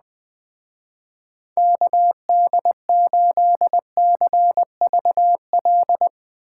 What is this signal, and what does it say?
Morse code, heard as a keyed tone, about 20 wpm, 700 Hz. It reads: KD8CVL
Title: Speed Racing - USA 2x3 Call Signs 17wpm